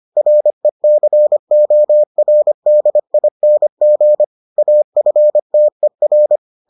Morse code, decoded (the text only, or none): RECORDING AFTER